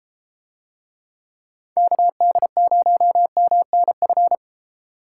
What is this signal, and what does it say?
Morse code, heard as a keyed tone, about 33 wpm, 700 Hz. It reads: KD0MNF